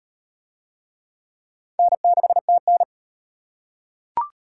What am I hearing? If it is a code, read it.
Morse code, heard as a keyed tone, about 38 wpm, 700 Hz. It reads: N6TN